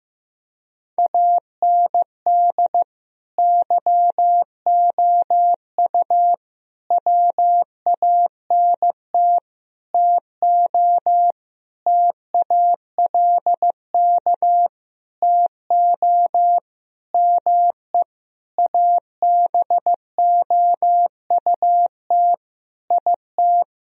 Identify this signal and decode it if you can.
Morse code, heard as a keyed tone, about 15 wpm, 700 Hz. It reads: AND YOU WANT TO TALK TO ME ABOUT IT
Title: Sentences from Top 1000 Words - Review Part 4 - 15wpm